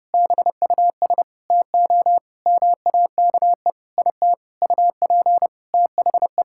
Morse code, heard as a keyed tone, about 32 wpm, 700 Hz. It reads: BUS TO MAKE IT UP THE
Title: Sentences from Top 1000 Words - Review Part 6 - 30wpm